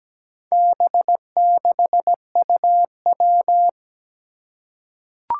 Morse code, heard as a keyed tone, about 17 wpm, 700 Hz. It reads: B6UW
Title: Four Challenging Letter-Number Combinations 17wpm